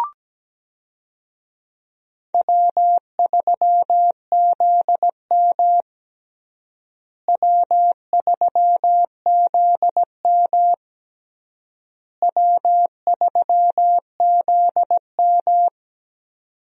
Morse code, heard as a keyed tone, about 17 wpm, 700 Hz. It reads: W3ZM W3ZM W3ZM
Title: USA Call Signs - 4 Characters - Repeated 3x - 17wpm